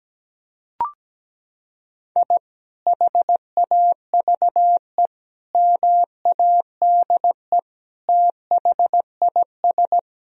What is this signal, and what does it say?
Morse code, heard as a keyed tone, about 17 wpm, 700 Hz. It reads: I HAVE MADE THIS
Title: Sentences from Top 1000 Words - Review Part 2 - 17wpm